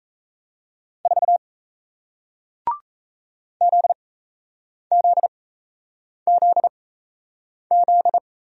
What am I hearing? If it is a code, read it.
Morse code, heard as a keyed tone, about 38 wpm, spaced as for 8 wpm, 700 Hz. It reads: 4 ZZZZ